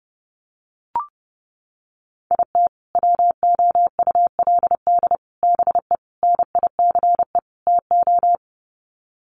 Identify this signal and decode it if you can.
Morse code, heard as a keyed tone, about 30 wpm, 700 Hz. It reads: IT WOULD BE NICE TO
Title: Sets of 5 Words Encore2 30wpm